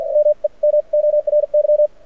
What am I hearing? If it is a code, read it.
Morse code, heard as a keyed tone, about 52 wpm, spaced as for 41 wpm, 590 Hz. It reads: MEMORY
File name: archived / CRSnet